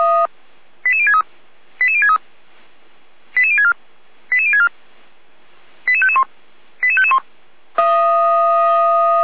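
Five-tone selective call (ZVEI), heard as quick runs of five tones, tones of about 70 ms. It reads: 88063 88063 88064 88064 88551 88551
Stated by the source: ZVEI five-tone call received with an SDR.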